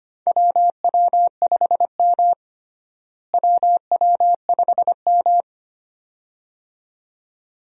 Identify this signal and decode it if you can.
Morse code, heard as a keyed tone, about 25 wpm, 700 Hz. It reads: WW5M WW5M
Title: USA Call Signs - 4 Characters - Repeated 2x - 25wpm